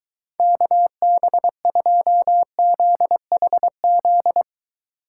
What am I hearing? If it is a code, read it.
Morse code, heard as a keyed tone, about 22 wpm, 700 Hz. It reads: KB2ZHZ